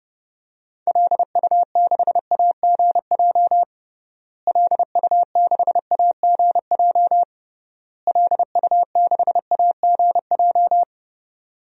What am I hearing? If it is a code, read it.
Morse code, heard as a keyed tone, about 30 wpm, 700 Hz. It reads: LU6AGJ LU6AGJ LU6AGJ
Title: DXCC-100 Call Signs - 6 Characters - Repeated 3x - 30wpm